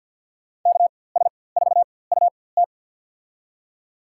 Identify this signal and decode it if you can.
Morse code, heard as a keyed tone, about 47 wpm, spaced as for 23 wpm, 700 Hz. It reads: KS4UT
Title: USA Call Signs - 5 Characters - 4x Character Spacing - 50wpm